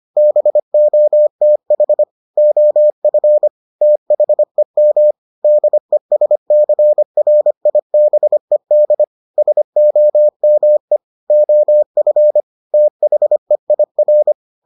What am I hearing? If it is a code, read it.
Morse code, heard as a keyed tone, about 25 wpm, 600 Hz. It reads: BOTH OF THEM DESCRIBED SOME OF THEIR